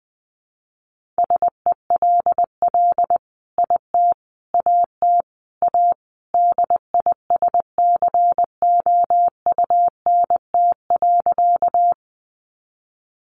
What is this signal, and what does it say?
Morse code, heard as a keyed tone, about 20 wpm, 700 Hz. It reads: SELL IT AT A DISCOUNT.
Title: Sentences from Top 1200 Words 20wpm